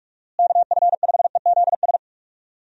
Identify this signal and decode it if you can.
Morse code, heard as a keyed tone, about 43 wpm, 700 Hz. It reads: KF5EZS
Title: USA Call Signs - 6 Characters - Repeated 3x - 45wpm